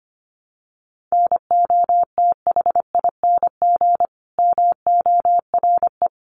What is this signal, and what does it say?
Morse code, heard as a keyed tone, about 25 wpm, 700 Hz. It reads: NOTHING MORE
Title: Sentences from Top 1100 Words 25wpm